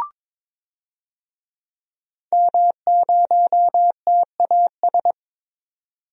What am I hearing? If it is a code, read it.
Morse code, heard as a keyed tone, about 22 wpm, 700 Hz. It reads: M0TAS
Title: DXCC-100 Call Signs - 5 Characters - 22wpm